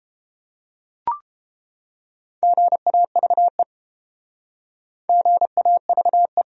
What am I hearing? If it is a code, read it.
Morse code, heard as a keyed tone, about 30 wpm, 700 Hz. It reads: GAVE GAVE